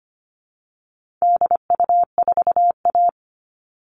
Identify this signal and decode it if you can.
Morse code, heard as a keyed tone, about 25 wpm, 700 Hz. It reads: DU4A
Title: All Cut Numbers - 4-Digit 25wpm